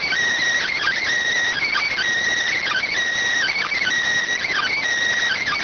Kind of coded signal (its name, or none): Emergency Alert System (SAME) tones